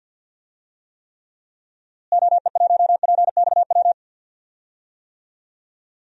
Morse code, heard as a keyed tone, about 49 wpm, 700 Hz. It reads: OE1PXW
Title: DXCC-100 Call Signs - 6 Characters - Repeated 2x - 50wpm